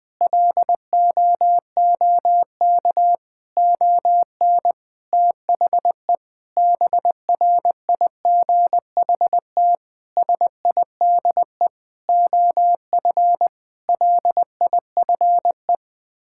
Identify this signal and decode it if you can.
Morse code, heard as a keyed tone, about 20 wpm, 700 Hz. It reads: LOOK ON THE BRIGHT SIDE OF LIFE
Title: Sets of 8 Words - 20wpm